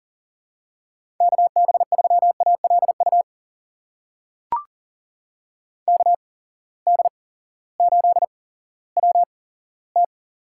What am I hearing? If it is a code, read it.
Morse code, heard as a keyed tone, about 37 wpm, spaced as for 18 wpm, 700 Hz. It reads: KB3ALU KD8WT
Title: USA Call Signs - 6 Characters - 8x Character Spacing - 40wpm